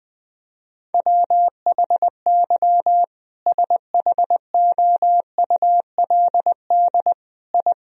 Morse code, heard as a keyed tone, about 20 wpm, 700 Hz. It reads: WHY SHOULD I
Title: Sentences from Top 1000 Words - Review Part 4 - 20wpm